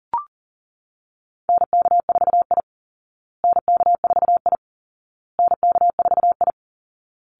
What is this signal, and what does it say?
Morse code, heard as a keyed tone, about 40 wpm, 700 Hz. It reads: NK4I NK4I NK4I
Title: Call Signs Repeated 2x - 40wpm